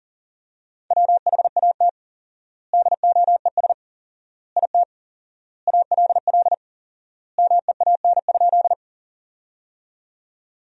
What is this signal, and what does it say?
Morse code, heard as a keyed tone, about 40 wpm, 700 Hz. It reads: WHAT DOES IT ALL MEAN?
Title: 4x Word-Spacing - Sentences from Top 200 Words - 40wpm